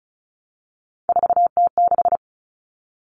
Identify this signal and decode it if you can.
Morse code, heard as a keyed tone, about 34 wpm, 700 Hz. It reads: 4T6
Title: All Cut Numbers - 3-Digit 35wpm